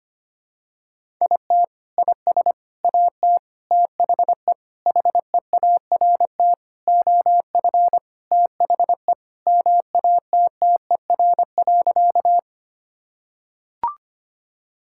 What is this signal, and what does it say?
Morse code, heard as a keyed tone, about 25 wpm, 700 Hz. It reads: IT IS AT THE HEART OF THE MATTER.
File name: practice / 25wpm / Sentences with 4-Word Phrases - 25wpm